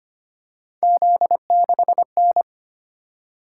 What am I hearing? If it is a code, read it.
Morse code, heard as a keyed tone, about 25 wpm, 700 Hz. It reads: Z6N
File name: Three Letter-Number Combinations - 25wpm